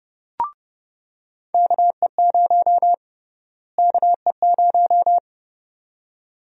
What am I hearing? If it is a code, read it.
Morse code, heard as a keyed tone, about 30 wpm, 700 Hz. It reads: KE0 KE0
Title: USA Call Sign Prefixes - Repeated 2x - 30wpm